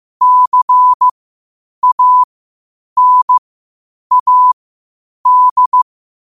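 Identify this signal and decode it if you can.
Morse code, heard as a keyed tone, about 15 wpm, spaced as for 8 wpm, 1000 Hz. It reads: CANAD
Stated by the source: Morse code at 8 wpm.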